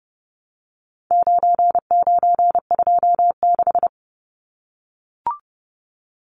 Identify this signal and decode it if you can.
Morse code, heard as a keyed tone, about 30 wpm, 700 Hz. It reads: 9926
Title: Numbers - 4-Digit 30wpm